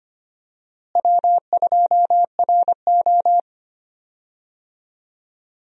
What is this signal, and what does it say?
Morse code, heard as a keyed tone, about 25 wpm, 700 Hz. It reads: W2RO
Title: USA Call Signs - 4 Characters - Repeated 3x - 25wpm